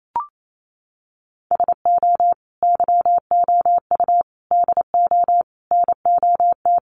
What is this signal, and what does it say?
Morse code, heard as a keyed tone, about 28 wpm, 700 Hz. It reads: SO YOU DO NOT
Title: Sets of 5 Words Encore 28wpm